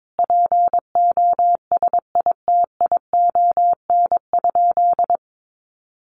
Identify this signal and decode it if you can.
Morse code, heard as a keyed tone, about 22 wpm, 700 Hz. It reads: POSITION?
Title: Sentences from Top 900 Words 22wpm